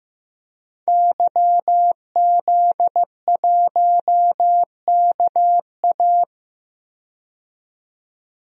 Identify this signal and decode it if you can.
Morse code, heard as a keyed tone, about 15 wpm, 700 Hz. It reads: YZ1KA
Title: DXCC-100 Call Signs - 5 Characters - 15wpm